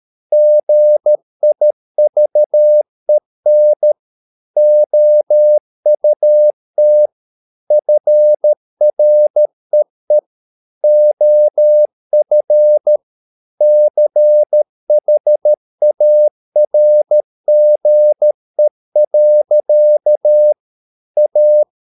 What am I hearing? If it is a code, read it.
Morse code, heard as a keyed tone, about 13 wpm, 600 Hz. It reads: GIVEN OUT FREE OF CHARGE. A
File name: mp3 files / Lexiphiles - 13wpm